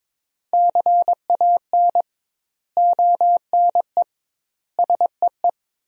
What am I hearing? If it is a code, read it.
Morse code, heard as a keyed tone, about 22 wpm, 700 Hz. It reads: CAN ONE SEE
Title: 2x Word-Spacing - Sentences from Top 100 Words - 22wpm